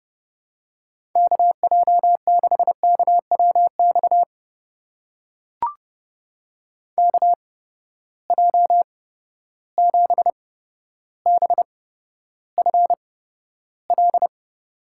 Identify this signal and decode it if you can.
Morse code, heard as a keyed tone, about 30 wpm, spaced as for 13 wpm, 700 Hz. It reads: KJ6KWX KJ7BFL
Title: USA Call Signs - 6 Characters - 8x Character Spacing - 30wpm